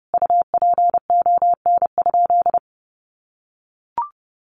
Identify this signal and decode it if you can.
Morse code, heard as a keyed tone, about 30 wpm, 700 Hz. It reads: UPON?
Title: Sentences from Top 1100 Words 30wpm